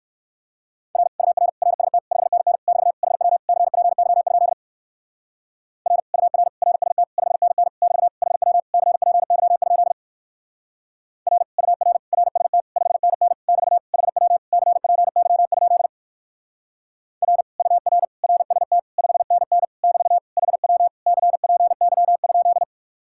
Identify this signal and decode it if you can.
Morse code, heard as a keyed tone, about 57 wpm, 700 Hz. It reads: R UR RST 5NN <BT> HW CPY? R UR RST 5NN <BT> HW CPY? R UR RST 5NN <BT> HW CPY? R UR RST 5NN <BT> HW CPY?